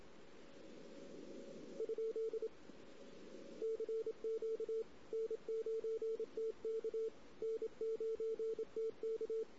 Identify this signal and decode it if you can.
Morse code, heard as a keyed tone, about 27 wpm, 440 Hz. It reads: ? CQ N9TK N9TK
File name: CW / 2008-ss-cw-k5tr-op-n5rz -16:00